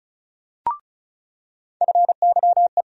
Morse code, heard as a keyed tone, about 36 wpm, 700 Hz. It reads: FYE
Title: USA Call Sign Postfixes - 3 Letters 35wpm